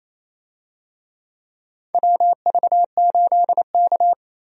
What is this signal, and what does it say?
Morse code, heard as a keyed tone, about 28 wpm, 700 Hz. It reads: WV8K
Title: USA Call Signs - 4 Characters - Repeated 3x - 28wpm